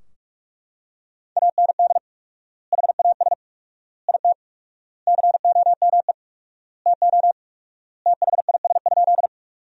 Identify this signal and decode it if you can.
Morse code, heard as a keyed tone, about 43 wpm, 700 Hz. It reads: AND HAS IT COME TO THIS?
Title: 4x Word-Spacing - Sentences from Top 100 Words - 45wpm